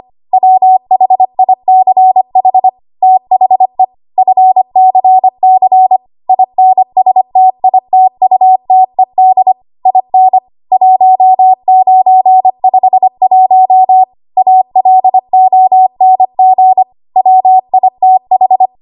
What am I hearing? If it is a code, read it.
Morse code, heard as a keyed tone, about 25 wpm, 750 Hz. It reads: WHICH THE FCC INSTITUTED IN 1951 ALONG WITH
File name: Archive / 25 WPM / 25WPM